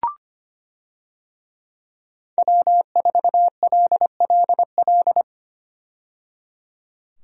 Morse code, heard as a keyed tone, about 25 wpm, 700 Hz. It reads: W4LLL
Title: USA Call Signs - 5 Characters - 25wpm